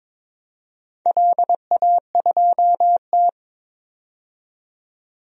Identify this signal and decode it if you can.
Morse code, heard as a keyed tone, about 21 wpm, 700 Hz. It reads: LA2T
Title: DXCC-100 Call Signs - 4 Characters - 22wpm